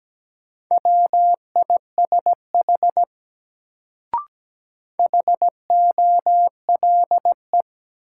Morse code, heard as a keyed tone, about 17 wpm, 700 Hz. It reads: WISH HOLE
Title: Sentences from Top 1600 Words - New Word Drill Alt - 17wpm